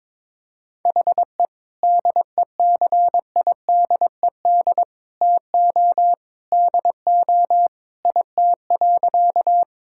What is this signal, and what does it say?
Morse code, heard as a keyed tone, about 22 wpm, 700 Hz. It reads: HE DECIDED TO DO IT.
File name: practice / 22wpm / Sentences with 3-Word Phrases - 22wpm